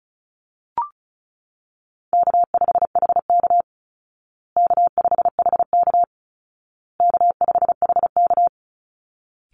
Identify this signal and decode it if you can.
Morse code, heard as a keyed tone, about 34 wpm, 700 Hz. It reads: K5HK K5HK K5HK
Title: Call Signs Repeated 2x - 35wpm